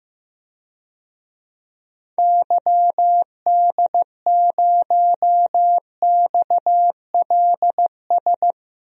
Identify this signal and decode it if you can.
Morse code, heard as a keyed tone, about 15 wpm, 700 Hz. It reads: YD0XLS
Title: DXCC-100 Call Signs - 6 Characters - 15wpm